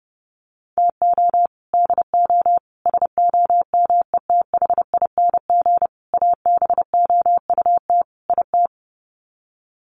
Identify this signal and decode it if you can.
Morse code, heard as a keyed tone, about 29 wpm, 700 Hz. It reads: TO DO SOMETHING ABOUT IT
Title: Sets of 5 Words 30wpm